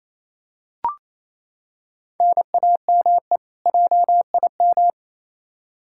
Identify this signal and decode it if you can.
Morse code, heard as a keyed tone, about 28 wpm, 700 Hz. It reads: NAME JIM
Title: K1USN SST QSO Practice 28wpm